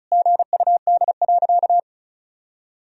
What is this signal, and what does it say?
Morse code, heard as a keyed tone, about 34 wpm, 700 Hz. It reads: GUD.
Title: Sentences from Top 200 Words and Abbreviations 35wpm